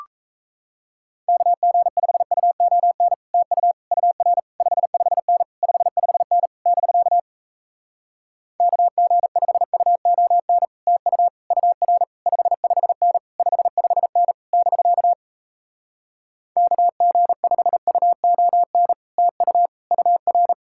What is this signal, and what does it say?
Morse code, heard as a keyed tone, about 37 wpm, 700 Hz. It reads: KG5UON TU UR 55N 55N <BK> KG5UON TU UR 55N 55N <BK> KG5UON TU UR